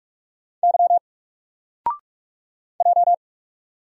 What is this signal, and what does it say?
Morse code, heard as a keyed tone, about 44 wpm, 700 Hz. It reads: Y J
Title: Single Character (Letter:Number:Punctuation:Prosign) Alt 45wpm